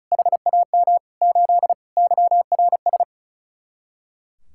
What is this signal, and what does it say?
Morse code, heard as a keyed tone, about 45 wpm, 700 Hz. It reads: HAM 8 YRS